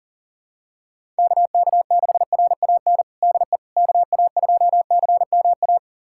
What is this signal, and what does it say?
Morse code, heard as a keyed tone, about 40 wpm, 700 Hz. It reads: KK6RAN DE KA2CMA
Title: USA Callsign DE Callsign - 6 Characters - 40wpm